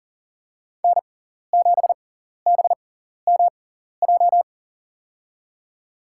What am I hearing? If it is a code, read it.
Morse code, heard as a keyed tone, about 40 wpm, spaced as for 14 wpm, 700 Hz. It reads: N7BMJ
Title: USA Call Signs - 5 Characters - 6x Character Spacing - 40wpm